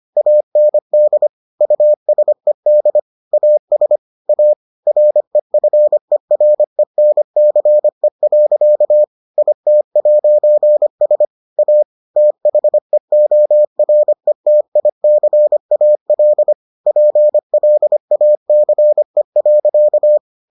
Morse code, heard as a keyed tone, about 25 wpm, 600 Hz. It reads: AND USED AS A REFERENCE. IT'S A THEORETICAL PLACE.